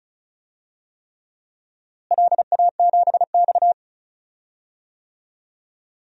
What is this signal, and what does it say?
Morse code, heard as a keyed tone, about 34 wpm, 700 Hz. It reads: LA7X
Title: DXCC-100 Call Signs - 4 Characters - Repeated 3x - 35wpm